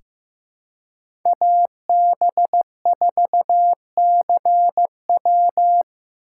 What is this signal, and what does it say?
Morse code, heard as a keyed tone, about 15 wpm, 700 Hz. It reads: AB4CW
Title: USA Call Signs - 5 Characters - 2x Character Spacing - 15wpm